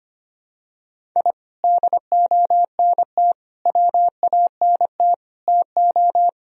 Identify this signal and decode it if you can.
Morse code, heard as a keyed tone, about 25 wpm, 700 Hz. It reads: I DONT WANT TO
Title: Sentences from Top 1000 Words - Review Part 4 - 25wpm